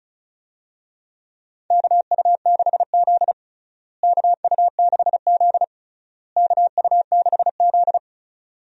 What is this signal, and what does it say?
Morse code, heard as a keyed tone, about 36 wpm, 700 Hz. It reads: KU6Z KU6Z KU6Z
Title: USA Call Signs - 4 Characters - Repeated 3x - 35wpm